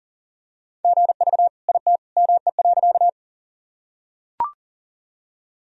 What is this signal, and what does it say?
Morse code, heard as a keyed tone, about 40 wpm, 700 Hz. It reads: GV IT ME.
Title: Sentences from Top 200 Words and Abbreviations 40wpm